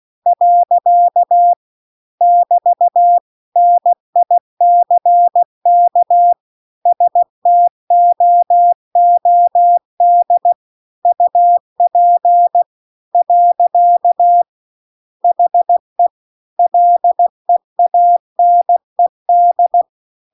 Morse code, heard as a keyed tone, about 16 wpm, 700 Hz. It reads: . <BT> NICK STOOD UP . HE LEANED